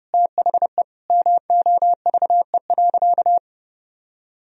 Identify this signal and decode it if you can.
Morse code, heard as a keyed tone, about 30 wpm, 700 Hz. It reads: THE MOVE.
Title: Sentences from Top 1000 Words - Review Part 1 - 30wpm